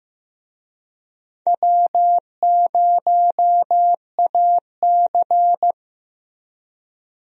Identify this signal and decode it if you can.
Morse code, heard as a keyed tone, about 15 wpm, 700 Hz. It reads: W0AC
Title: USA Call Signs - 4 Characters - Repeated 2x - 15wpm